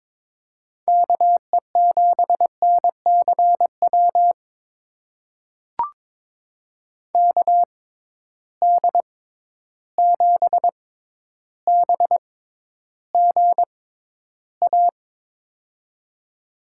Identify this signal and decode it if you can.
Morse code, heard as a keyed tone, about 22 wpm, spaced as for 11 wpm, 700 Hz. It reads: KE7NCW KD7BGA
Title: USA Call Signs - 6 Characters - 6x Character Spacing - 22wpm